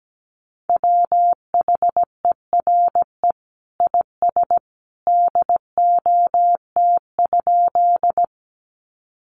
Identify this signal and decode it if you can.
Morse code, heard as a keyed tone, about 17 wpm, 700 Hz. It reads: WHERE IS DOT?
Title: Sentences from Top 1500 Words 17wpm